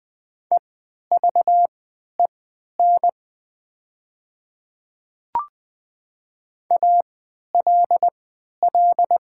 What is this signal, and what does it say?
Morse code, heard as a keyed tone, about 20 wpm, spaced as for 11 wpm, 700 Hz. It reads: EVEN ALL
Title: 3x Character Spacing - Top 200 Words - 20wpm